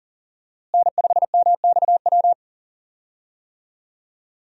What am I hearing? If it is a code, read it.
Morse code, heard as a keyed tone, about 37 wpm, 700 Hz. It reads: N5MXW
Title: USA Call Signs - 5 Characters - Repeated 3x - 40wpm